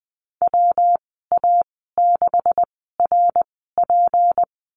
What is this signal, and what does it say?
Morse code, heard as a keyed tone, about 20 wpm, spaced as for 15 wpm, 700 Hz. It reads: WA6RP
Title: 167 Lesson 27g - Number 7 - Call Signs 20wpm